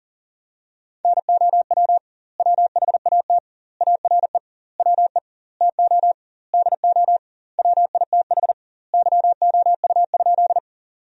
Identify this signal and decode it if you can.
Morse code, heard as a keyed tone, about 37 wpm, 700 Hz. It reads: NOW WHAT ARE WE TO DO WITH YOU?
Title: 2x Word-Spacing - Sentences from Top 100 Words - 40wpm